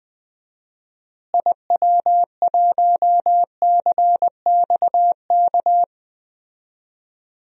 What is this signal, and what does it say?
Morse code, heard as a keyed tone, about 20 wpm, 700 Hz. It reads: IW1CXK
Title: DXCC-100 Call Signs - 6 Characters - Repeated 2x - 20wpm